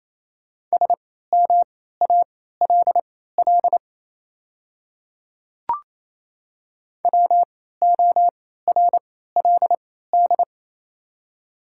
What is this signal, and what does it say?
Morse code, heard as a keyed tone, about 28 wpm, spaced as for 16 wpm, 700 Hz. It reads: SMALL WORLD
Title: 3x Character Spacing - Top 200 Words - 28wpm